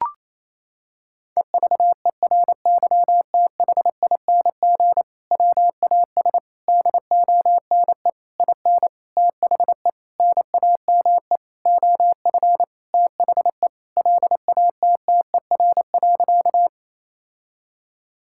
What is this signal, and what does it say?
Morse code, heard as a keyed tone, about 28 wpm, 700 Hz. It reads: EVERYTHING WAS DONE IN THE NAME OF THE LATTER.
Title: Sentences from Top 1600 Words 28wpm